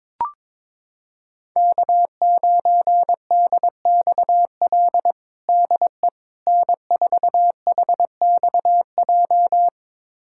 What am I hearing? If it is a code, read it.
Morse code, heard as a keyed tone, about 22 wpm, 700 Hz. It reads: K9DXL DE N4HXJ
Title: USA Callsign DE Callsign - 5 Characters - 22wpm